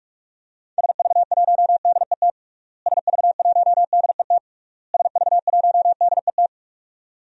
Morse code, heard as a keyed tone, about 47 wpm, 700 Hz. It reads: SV1DET SV1DET SV1DET
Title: DXCC-100 Call Signs - 6 Characters - Repeated 3x - 45wpm